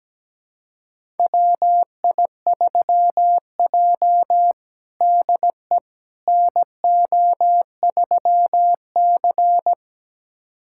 Morse code, heard as a keyed tone, about 17 wpm, 700 Hz. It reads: WI3J DE NO3C
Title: USA Callsign DE Callsign - 4 Characters - 17wpm